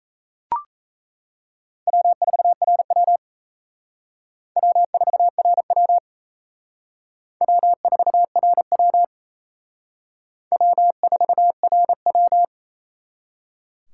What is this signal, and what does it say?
Morse code, heard as a keyed tone, about 34 wpm, 700 Hz. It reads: W4RW W4RW W4RW W4RW
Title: Speed Racing - USA 1x2 Call Signs 28wpm